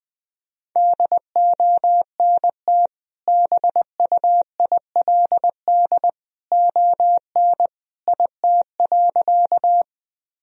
Morse code, heard as a keyed tone, about 20 wpm, 700 Hz. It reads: DONT BUILD ON IT.
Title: Sentences from Top 1000 Words - Review Part 1 - 20wpm